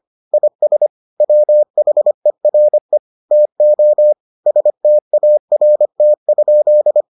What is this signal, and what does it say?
Morse code, heard as a keyed tone, about 25 wpm, 600 Hz. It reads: IS WHERE TO START?